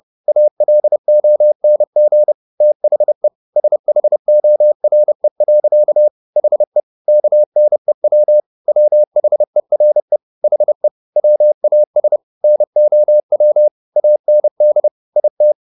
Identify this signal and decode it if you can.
Morse code, heard as a keyed tone, about 30 wpm, 600 Hz. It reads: ALONG THE SHORE. HE KNEW WHERE HE WAS NOW AND IT